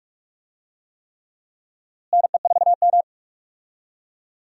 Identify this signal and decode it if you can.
Morse code, heard as a keyed tone, about 42 wpm, 700 Hz. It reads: NE4M
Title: USA Call Signs - 4 Characters - Repeated 3x - 45wpm